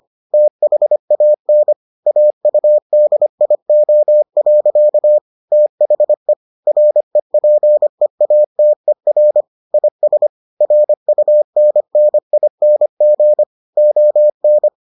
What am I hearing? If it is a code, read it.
Morse code, heard as a keyed tone, about 25 wpm, 600 Hz. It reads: THAN AUDIO. THE REPEATER IS RUNNING ON